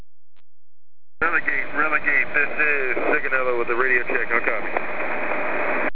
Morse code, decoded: none

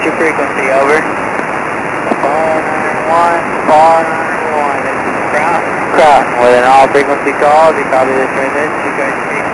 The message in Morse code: none

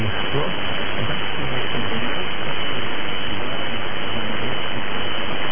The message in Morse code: none

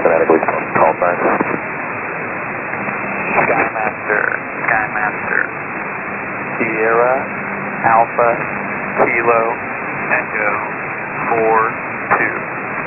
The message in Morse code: none

